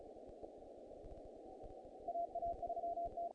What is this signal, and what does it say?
Morse code, heard as a keyed tone, about 34 wpm, 690 Hz, with noise. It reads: AA3T
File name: contests / naqp-cw-k5tr-op-k5pi- -1800